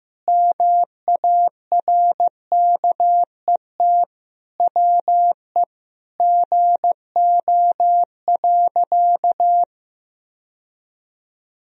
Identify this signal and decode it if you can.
Morse code, heard as a keyed tone, about 15 wpm, 700 Hz. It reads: MARKET WE GO.
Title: Sentences from Top 1000 Words - Review Part 3 - 15wpm